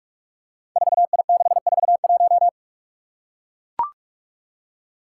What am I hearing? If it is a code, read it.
Morse code, heard as a keyed tone, about 44 wpm, 700 Hz. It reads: 4I641
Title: Five Challenging Letter-Number Combinations 45wpm